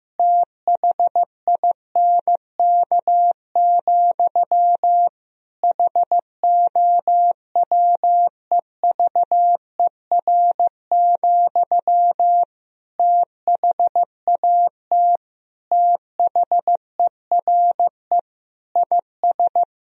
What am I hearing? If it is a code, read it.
Morse code, heard as a keyed tone, about 15 wpm, 700 Hz. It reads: THINK, HOWEVER, THAT THERE IS
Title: Sentences from Top 1000 Words - Review Part 3 - 15wpm